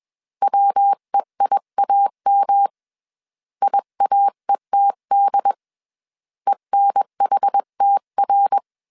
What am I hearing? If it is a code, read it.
Morse code, heard as a keyed tone, about 21 wpm, 800 Hz. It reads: WEIAM IAETD ENHTR
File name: CW / DET / 1 3 1500C S B 20wpm 1754